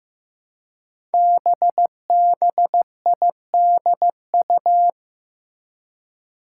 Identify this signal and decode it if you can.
Morse code, heard as a keyed tone, about 15 wpm, 700 Hz. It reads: BBIDU
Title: Five Challenging Letter-Number Combinations 15wpm